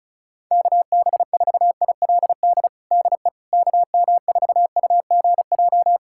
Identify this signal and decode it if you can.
Morse code, heard as a keyed tone, about 36 wpm, 700 Hz. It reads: KB4ILD DE KM4UGJ
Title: USA Callsign DE Callsign - 6 Characters - 35wpm